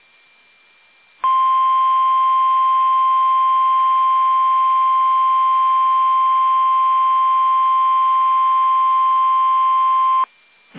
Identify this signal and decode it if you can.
Emergency Alert System (SAME) tones (the 1050 Hz attention tone). It is heard from 1.2 to 10.3 s.
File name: eas / KUSO / KUSO